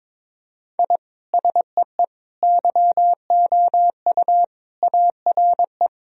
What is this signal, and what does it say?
Morse code, heard as a keyed tone, about 22 wpm, 700 Hz. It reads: I SEE YOU ARE
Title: Sentences from Top 1000 Words - Review Part 3 - 22wpm